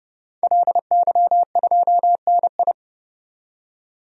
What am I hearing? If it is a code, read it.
Morse code, heard as a keyed tone, about 32 wpm, 700 Hz. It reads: LY2NI
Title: DXCC-100 Call Signs - 5 Characters - 30wpm